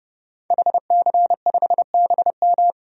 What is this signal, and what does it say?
Morse code, heard as a keyed tone, about 32 wpm, 700 Hz. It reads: HC5BM
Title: DXCC-100 Call Signs - 5 Characters - 30wpm